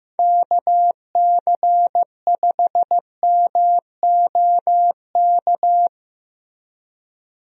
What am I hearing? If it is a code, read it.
Morse code, heard as a keyed tone, about 15 wpm, 700 Hz. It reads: KC5MOK
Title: USA Call Signs - 6 Characters - Repeated 3x - 15wpm